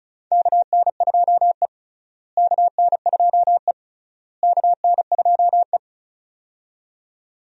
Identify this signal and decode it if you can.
Morse code, heard as a keyed tone, about 34 wpm, 700 Hz. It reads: KN2E KN2E KN2E
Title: USA Call Signs - 4 Characters - Repeated 3x - 35wpm